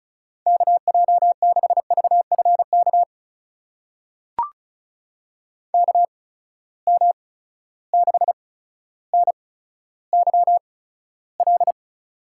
Morse code, heard as a keyed tone, about 36 wpm, spaced as for 15 wpm, 700 Hz. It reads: KJ6VFK KM6NYL
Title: USA Call Signs - 6 Characters - 8x Character Spacing - 35wpm